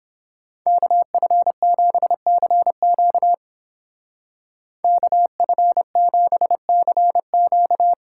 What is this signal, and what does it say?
Morse code, heard as a keyed tone, about 29 wpm, 700 Hz. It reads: KF7CQ KF7CQ